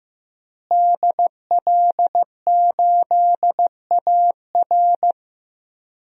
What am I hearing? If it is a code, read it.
Morse code, heard as a keyed tone, about 15 wpm, 700 Hz. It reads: DL8AR
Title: DXCC-100 Call Signs - 5 Characters - Repeated 3x - 15wpm